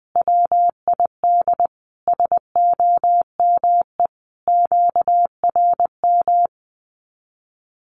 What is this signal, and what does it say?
Morse code, heard as a keyed tone, about 20 wpm, 700 Hz. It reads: WID SOME QRM
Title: Common QSO Phrases 20wpm